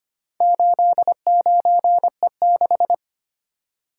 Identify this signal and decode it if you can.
Morse code, heard as a keyed tone, about 25 wpm, 700 Hz. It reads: 89E6
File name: practice / 25wpm / Four Letter-Number Combinations - 25wpm